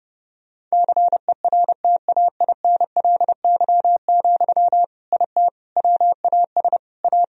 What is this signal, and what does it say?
Morse code, heard as a keyed tone, about 30 wpm, 700 Hz. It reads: CERTAINLY, IT WAS A
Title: Sentences from Top 1000 Words - Review Part 2 - 30wpm